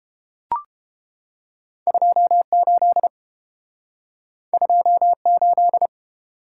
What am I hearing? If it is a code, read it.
Morse code, heard as a keyed tone, about 30 wpm, 700 Hz. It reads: 28 28